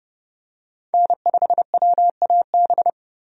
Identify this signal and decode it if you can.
Morse code, heard as a keyed tone, about 31 wpm, 700 Hz. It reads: N5WAB
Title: USA Call Signs - 5 Characters - 30wpm